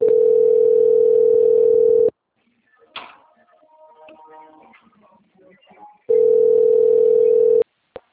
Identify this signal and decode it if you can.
Ringback tone (North America), 440 + 480 Hz.
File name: out- - - - - .27943